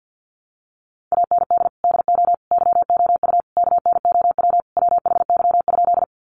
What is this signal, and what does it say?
Morse code, heard as a keyed tone, about 50 wpm, 700 Hz. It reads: AND DO YOU KNOW WHY?
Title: Sentences from Top 200 Words 50wpm